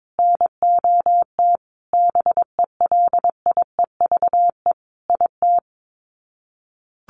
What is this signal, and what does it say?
Morse code, heard as a keyed tone, about 22 wpm, 700 Hz. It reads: NOT BELIEVE IT
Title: Sets of 5 Words Encore 22wpm